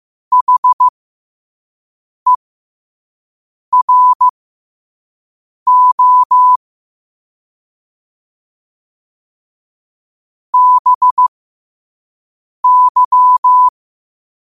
Morse code, heard as a keyed tone, about 15 wpm, spaced as for 5 wpm, 1000 Hz. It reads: HERO BY
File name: qotdfiles / 5wpm